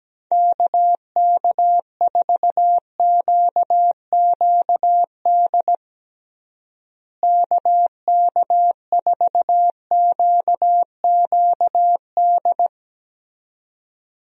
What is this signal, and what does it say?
Morse code, heard as a keyed tone, about 17 wpm, 700 Hz. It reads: KK4QQD KK4QQD
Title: USA Call Signs - 6 Characters - Repeated 2x - 17wpm